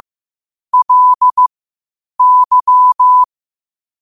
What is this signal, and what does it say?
Morse code, heard as a keyed tone, about 15 wpm, spaced as for 8 wpm, 1000 Hz. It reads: LY
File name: qotdfiles / 8wpm